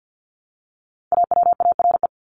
Morse code, heard as a keyed tone, about 50 wpm, 700 Hz. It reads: AWARE
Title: Top 1001-1250 Words 50wpm